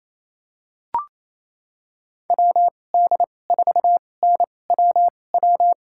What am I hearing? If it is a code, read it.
Morse code, heard as a keyed tone, about 28 wpm, spaced as for 20 wpm, 700 Hz. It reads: WD4NWW
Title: USA Call Signs - 6 Characters - 2x Character Spacing - 28wpm